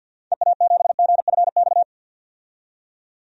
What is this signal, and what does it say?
Morse code, heard as a keyed tone, about 52 wpm, 700 Hz. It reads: EA7GFX
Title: DXCC-100 Call Signs - 6 Characters - 50wpm